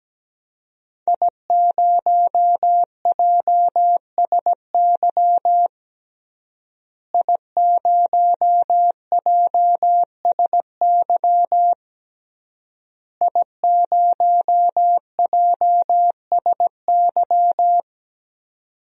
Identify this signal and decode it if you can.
Morse code, heard as a keyed tone, about 17 wpm, 700 Hz. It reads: I0JSY I0JSY I0JSY
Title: DXCC-100 Call Signs - 5 Characters - Repeated 3x - 17wpm